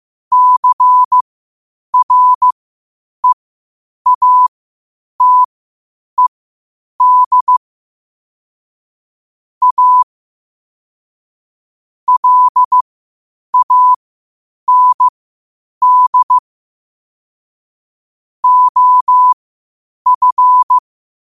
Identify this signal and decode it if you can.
Morse code, heard as a keyed tone, about 15 wpm, spaced as for 8 wpm, 1000 Hz. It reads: CREATED A LAND OF
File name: qotdfiles / 8wpm